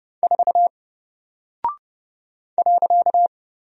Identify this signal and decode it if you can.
Morse code, heard as a keyed tone, about 31 wpm, 700 Hz. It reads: 4 .
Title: Single Character (Letter:Number:Punctuation:Prosign) Alt 30wpm